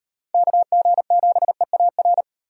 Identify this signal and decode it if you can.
Morse code, heard as a keyed tone, about 38 wpm, 700 Hz. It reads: KG7EAR